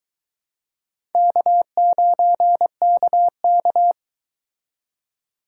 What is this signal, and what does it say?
Morse code, heard as a keyed tone, about 22 wpm, 700 Hz. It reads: K9KK